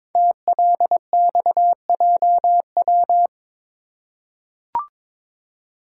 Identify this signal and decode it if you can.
Morse code, heard as a keyed tone, about 22 wpm, 700 Hz. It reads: TLXJW
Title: Code Groups - 5 Letters - 22wpm